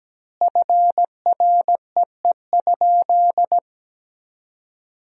Morse code, heard as a keyed tone, about 17 wpm, 700 Hz. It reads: FREE?
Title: 4x Word-Spacing - Sentences from Top 500 Words - 17wpm